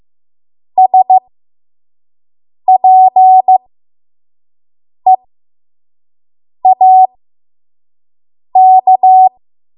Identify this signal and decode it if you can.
Morse code, heard as a keyed tone, about 15 wpm, spaced as for 5 wpm, 750 Hz. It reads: SPEAK
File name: Morse / 05WPM